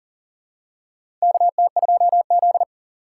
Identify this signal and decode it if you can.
Morse code, heard as a keyed tone, about 39 wpm, 700 Hz. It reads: KT2Z
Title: USA Call Signs - 4 Characters - 2x Character Spacing - 40wpm